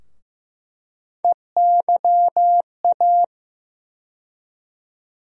Morse code, heard as a keyed tone, about 15 wpm, 700 Hz. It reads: EYA
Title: USA Call Sign Postfixes - 3 Letters 15wpm